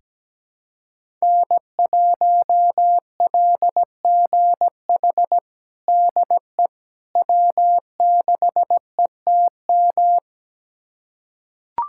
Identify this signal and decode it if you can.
Morse code, heard as a keyed tone, about 17 wpm, 700 Hz. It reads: N1LGH DE W6ETM
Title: USA Callsign DE Callsign - 5 Characters - 17wpm